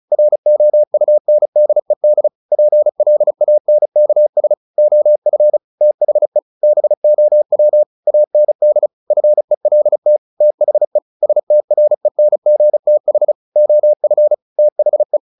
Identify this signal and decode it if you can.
Morse code, heard as a keyed tone, about 35 wpm, 600 Hz. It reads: ROUNDED PLANKS OF THE BOW AND FELT THE STRENGTH OF THE
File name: the old man and the sea / the old man and the sea 1 35wpm